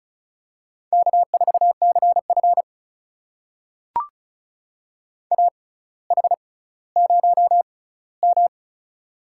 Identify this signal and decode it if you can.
Morse code, heard as a keyed tone, about 36 wpm, spaced as for 18 wpm, 700 Hz. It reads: K4CF AH0M
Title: USA Call Signs - 4 Characters - 6x Character Spacing - 35wpm